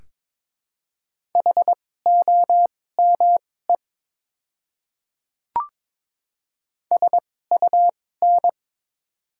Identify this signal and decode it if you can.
Morse code, heard as a keyed tone, about 21 wpm, spaced as for 16 wpm, 700 Hz. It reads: HOME SUN
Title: 2x Character Spacing - Top 300 Words - 22wpm